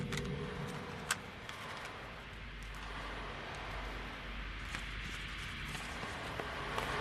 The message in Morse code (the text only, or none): none